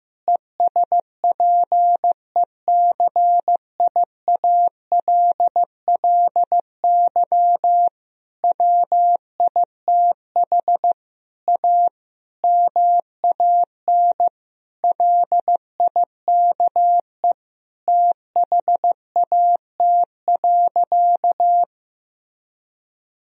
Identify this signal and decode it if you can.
Morse code, heard as a keyed tone, about 14 wpm, 700 Hz. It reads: ESPECIALLY WITH A MAN LIKE THAT.
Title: Sentences from Top 1000 Words - Review Part 4 - 15wpm